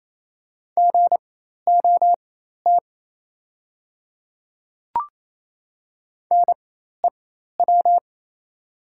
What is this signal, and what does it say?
Morse code, heard as a keyed tone, about 28 wpm, spaced as for 13 wpm, 700 Hz. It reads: GOT NEW
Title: 4x Character Spacing - Top 300 Words - 28wpm